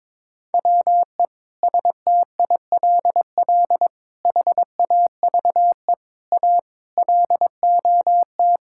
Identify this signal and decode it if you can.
Morse code, heard as a keyed tone, about 22 wpm, 700 Hz. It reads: WE STILL HAVE A LOT
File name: Sets of 9 Words - 22wpm